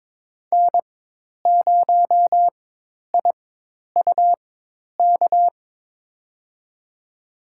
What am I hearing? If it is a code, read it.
Morse code, heard as a keyed tone, about 22 wpm, spaced as for 10 wpm, 700 Hz. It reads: N0IUK
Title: USA Call Signs - 5 Characters - 4x Character Spacing - 22wpm